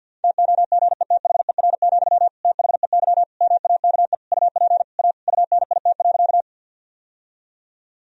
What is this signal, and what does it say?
Morse code, heard as a keyed tone, about 52 wpm, 700 Hz. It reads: TOGETHER, THEY MAKE UP A UNIT.
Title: Sentences from Top 1000 Words - Review Part 2 - 50wpm